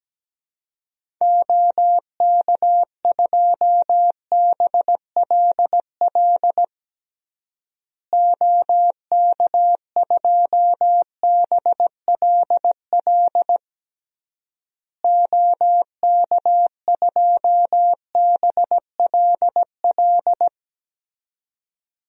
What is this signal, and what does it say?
Morse code, heard as a keyed tone, about 17 wpm, 700 Hz. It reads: OK2BLL OK2BLL OK2BLL
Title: DXCC-100 Call Signs - 6 Characters - Repeated 3x - 17wpm